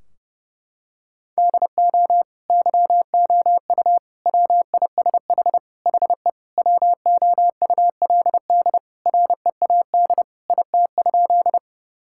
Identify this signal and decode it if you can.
Morse code, heard as a keyed tone, about 32 wpm, 700 Hz. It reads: DO YOU WISH HE WOULD READ IT?
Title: Sentences from Top 1600 Words 30wpm